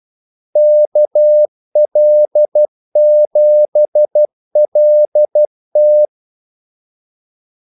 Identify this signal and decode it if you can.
Morse code, heard as a keyed tone, about 12 wpm, 600 Hz. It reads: KL7LT
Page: 2 ekstra ordmellemrum mellem kaldesignalerne